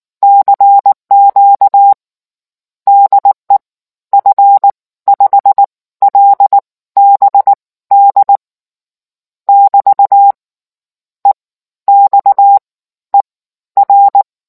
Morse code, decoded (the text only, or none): CQ DE F5LBD <BT> EXER